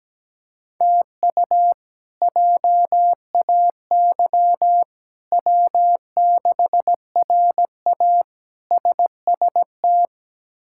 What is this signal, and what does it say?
Morse code, heard as a keyed tone, about 17 wpm, 700 Hz. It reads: TU JAY W6RA SST
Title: K1USN SST QSO Practice - 17wpm